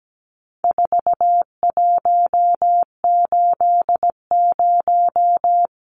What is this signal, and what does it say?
Morse code, heard as a keyed tone, about 17 wpm, 700 Hz. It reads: 4180
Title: Numbers - 4-Digit 17wpm